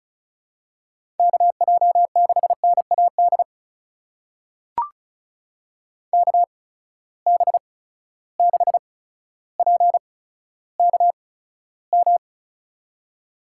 Morse code, heard as a keyed tone, about 36 wpm, spaced as for 15 wpm, 700 Hz. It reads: KJ6NAD KB6PKM
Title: USA Call Signs - 6 Characters - 8x Character Spacing - 35wpm